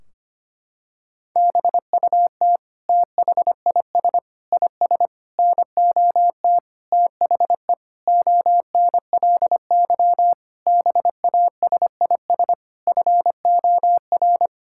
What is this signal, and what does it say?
Morse code, heard as a keyed tone, about 25 wpm, 700 Hz. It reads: BUT THIS IS NOT THE ONLY BASIS FOR
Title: Sentences from Top 1600 Words 25wpm